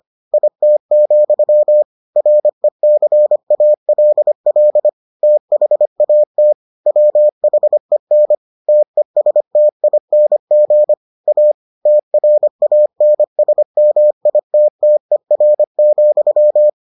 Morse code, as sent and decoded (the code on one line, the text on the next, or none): .. - --..-- / .-. -.-. .- .-.. .-.. / - .... .- - / .-- .... -. / - . ... - .. -. --. / .- / - .-. .- -. ... -- .. - - . .-. --..--
IT, RECALL THAT WHEN TESTING A TRANSMITTER,